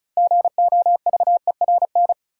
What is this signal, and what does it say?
Morse code, heard as a keyed tone, about 36 wpm, 700 Hz. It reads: GOVERN
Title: Sentences from Top 1000 Words - Review Part 2 - New Word Drill - 35wpm